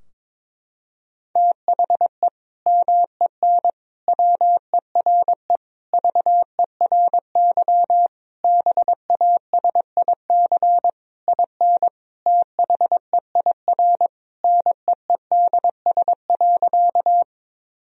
Morse code, heard as a keyed tone, about 21 wpm, 700 Hz. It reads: THE MEN WERE VERY BASIC IN THEIR NEEDS.
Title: Sentences from Top 1000 Words - Review Part 3 - 22wpm